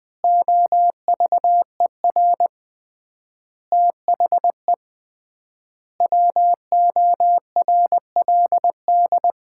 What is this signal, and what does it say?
Morse code, heard as a keyed tone, about 20 wpm, 700 Hz. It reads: OVER THE WORLD
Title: 3x Word-Spacing - Sets of 5 Words - 20wpm